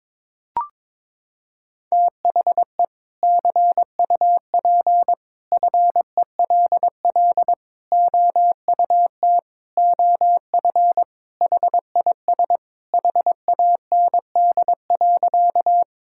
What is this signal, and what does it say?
Morse code, heard as a keyed tone, about 22 wpm, 700 Hz. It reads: THE CUP FELL OUT OF HIS HAND.
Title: Sentences from Top 1600 Words 22wpm